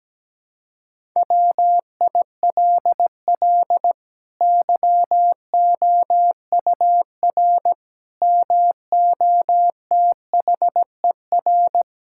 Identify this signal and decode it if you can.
Morse code, heard as a keyed tone, about 17 wpm, 700 Hz. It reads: WILL YOUR MOTHER
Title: Sentences from Top 1000 Words - Review Part 5 - 17wpm